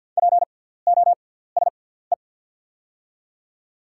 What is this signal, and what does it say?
Morse code, heard as a keyed tone, about 48 wpm, spaced as for 17 wpm, 690 Hz. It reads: POSE
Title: 6x Character Spacing - Top 500 Words - 50wpm